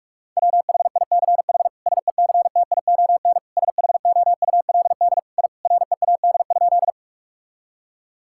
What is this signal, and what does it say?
Morse code, heard as a keyed tone, about 46 wpm, 700 Hz. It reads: WHICH SECTION SHOULD I READ?
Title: Sentences from Top 1000 Words - Review Part 5 - 45wpm